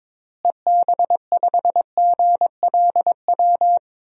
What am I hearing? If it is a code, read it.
Morse code, heard as a keyed tone, about 22 wpm, 700 Hz. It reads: EB5GLW
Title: DXCC-100 Call Signs - 6 Characters - Repeated 3x - 22wpm